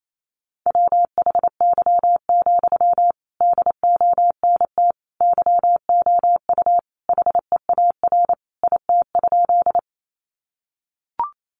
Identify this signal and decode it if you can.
Morse code, heard as a keyed tone, about 28 wpm, 700 Hz. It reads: WHY, DONT YOU HEAR IT?
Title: Sentences from Top 300 Words 28wpm